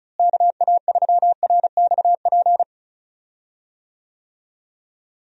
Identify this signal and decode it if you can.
Morse code, heard as a keyed tone, about 36 wpm, 700 Hz. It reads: KA3RXP
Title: USA Call Signs - 6 Characters - Repeated 3x - 35wpm